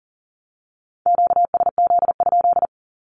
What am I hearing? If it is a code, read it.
Morse code, heard as a keyed tone, about 41 wpm, 700 Hz. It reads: QSZ?
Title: Q-Codes with Spoken Meaning - 40wpm